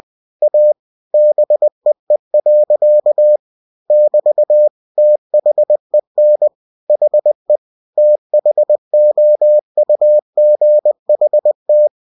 Morse code, heard as a keyed tone, about 20 wpm, 600 Hz. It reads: A BEE. <BT> THEN HE THOUGHT
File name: cw-books / winnie-the-pooh / winnie-20wpm-page-0002